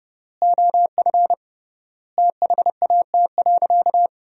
Morse code, heard as a keyed tone, about 30 wpm, 700 Hz. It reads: OF THAT.
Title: 3x Word-Spacing - Sentences from Top 300 Words - 30wpm